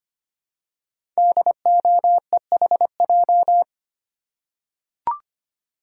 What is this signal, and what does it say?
Morse code, heard as a keyed tone, about 25 wpm, 700 Hz. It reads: DOEHJ
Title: Code Groups - 5 Letters - 25wpm